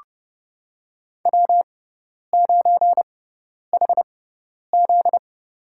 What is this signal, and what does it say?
Morse code, heard as a keyed tone, about 30 wpm, spaced as for 10 wpm, 690 Hz. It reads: W9HZ
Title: USA Call Signs - 4 Characters - 6x Character Spacing - 30wpm